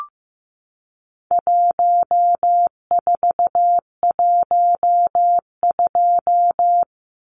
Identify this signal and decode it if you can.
Morse code, heard as a keyed tone, about 15 wpm, 700 Hz. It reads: 1412
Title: Numbers - 4-Digit 15wpm